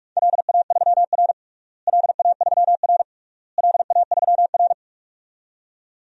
Morse code, heard as a keyed tone, about 46 wpm, 700 Hz. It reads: LA3R LA3R LA3R
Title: DXCC-100 Call Signs - 4 Characters - Repeated 3x - 45wpm